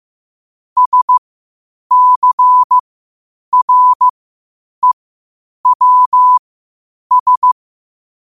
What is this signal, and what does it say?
Morse code, heard as a keyed tone, about 15 wpm, spaced as for 8 wpm, 1000 Hz. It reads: SCREWS
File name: qotdfiles / 8wpm